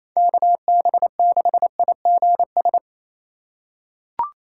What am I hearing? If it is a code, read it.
Morse code, heard as a keyed tone, about 28 wpm, 690 Hz. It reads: KB6IGS
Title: USA Call Signs - 2x Character Spacing - 28wpm